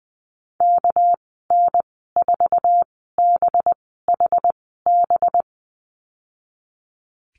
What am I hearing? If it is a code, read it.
Morse code, heard as a keyed tone, about 20 wpm, spaced as for 14 wpm, 700 Hz. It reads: KN4BHB
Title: 238 Lesson 38g - Letter q - Call Signs 20wpm